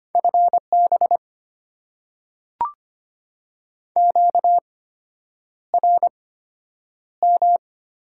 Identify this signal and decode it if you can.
Morse code, heard as a keyed tone, about 25 wpm, spaced as for 9 wpm, 690 Hz. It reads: FB QRM
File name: Top 90 QSO Elements - 8x Character Spacing - 25wpm